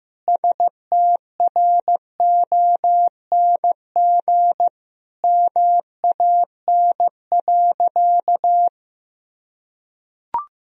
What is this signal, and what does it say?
Morse code, heard as a keyed tone, about 16 wpm, 700 Hz. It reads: STRONG MAN.
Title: Sentences from Top 1000 Words - Review Part 3 - 15wpm